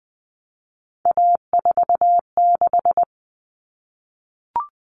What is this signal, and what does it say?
Morse code, heard as a keyed tone, about 20 wpm, 700 Hz. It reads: A46
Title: Common Cut Numbers - 3-Digit 20wpm